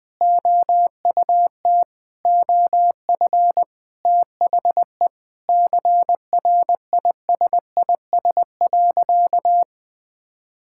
Morse code, heard as a keyed tone, about 20 wpm, 700 Hz. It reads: OUT OF THE CRISIS.
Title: Sentences with 8-Word Phrases - 20wpm